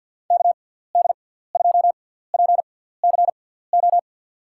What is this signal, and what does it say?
Morse code, heard as a keyed tone, about 52 wpm, spaced as for 17 wpm, 700 Hz. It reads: KD2PCO
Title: USA Call Signs - 6 Characters - 6x Character Spacing - 50wpm